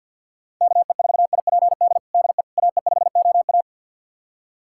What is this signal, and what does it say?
Morse code, heard as a keyed tone, about 51 wpm, 700 Hz. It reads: KE4IPD DE AE5OA
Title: USA Callsign DE Callsign 50wpm